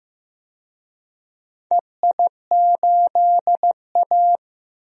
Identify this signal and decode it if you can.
Morse code, heard as a keyed tone, about 15 wpm, 700 Hz. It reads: EI8A
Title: DXCC-100 Call Signs - 4 Characters - 15wpm